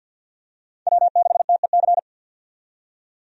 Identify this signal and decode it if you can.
Morse code, heard as a keyed tone, about 51 wpm, 700 Hz. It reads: W6TEC
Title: USA Call Signs - 5 Characters - 50wpm